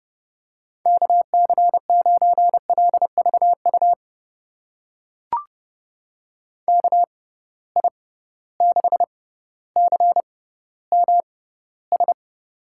Morse code, heard as a keyed tone, about 30 wpm, spaced as for 15 wpm, 700 Hz. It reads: KC9LVU KI6CMS
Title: USA Call Signs - 6 Characters - 6x Character Spacing - 30wpm